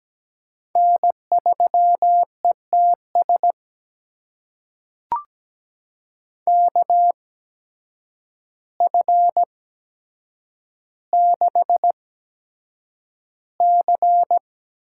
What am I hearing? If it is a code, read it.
Morse code, heard as a keyed tone, about 17 wpm, spaced as for 8 wpm, 700 Hz. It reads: N3ETS KF6C
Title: USA Call Signs - 5 Characters - 8x Character Spacing - 17wpm